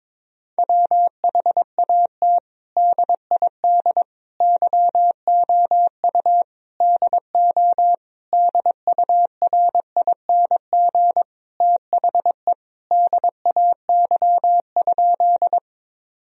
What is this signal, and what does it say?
Morse code, heard as a keyed tone, about 22 wpm, 700 Hz. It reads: WHAT DID YOU DO DURING THE DAY?
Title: Sentences from Top 1000 Words - Review Part 2 - 22wpm